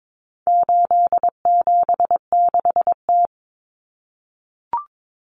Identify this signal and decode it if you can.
Morse code, heard as a keyed tone, about 22 wpm, 700 Hz. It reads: 876T
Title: Common Cut Numbers - 4-Digit 22wpm